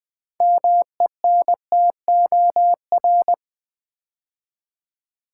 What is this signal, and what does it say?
Morse code, heard as a keyed tone, about 20 wpm, 700 Hz. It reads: MENTOR
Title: 6 Letter Words - First 3 Letters Given - 20wpm